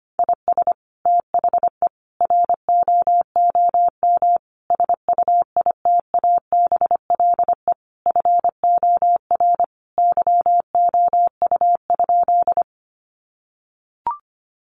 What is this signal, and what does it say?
Morse code, heard as a keyed tone, about 25 wpm, 700 Hz. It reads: IS THE ROOM SUITABLE FOR YOU?
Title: Sentences from Top 1400 Words 25wpm